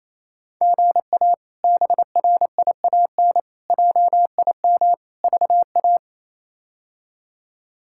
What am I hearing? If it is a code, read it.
Morse code, heard as a keyed tone, about 28 wpm, 700 Hz. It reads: GA BRIAN JIM VA
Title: K1USN SST QSO Practice - 28wpm